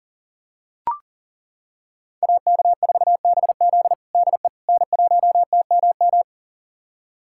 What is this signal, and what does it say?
Morse code, heard as a keyed tone, about 40 wpm, 700 Hz. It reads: AK4BZ DE N1TMM
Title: USA Callsign DE Callsign - 5 Characters - 40wpm